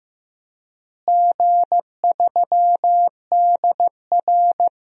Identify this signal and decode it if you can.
Morse code, heard as a keyed tone, about 15 wpm, 700 Hz. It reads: G3DR
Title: DXCC-100 Call Signs - 4 Characters - Repeated 3x - 15wpm